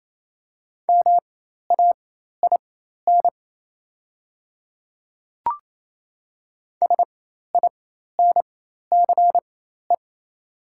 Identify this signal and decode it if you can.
Morse code, heard as a keyed tone, about 27 wpm, spaced as for 13 wpm, 700 Hz. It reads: MAIN SINCE
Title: 4x Character Spacing - Top 300 Words - 28wpm